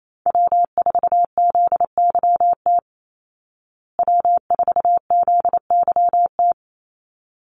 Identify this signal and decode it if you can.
Morse code, heard as a keyed tone, about 28 wpm, 690 Hz. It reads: W4ZYT W4ZYT
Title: Call Signs Repeated 3x - 28wpm